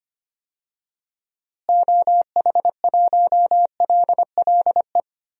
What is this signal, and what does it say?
Morse code, heard as a keyed tone, about 25 wpm, 700 Hz. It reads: OH1LLE
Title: DXCC-100 Call Signs - 6 Characters - Repeated 2x - 25wpm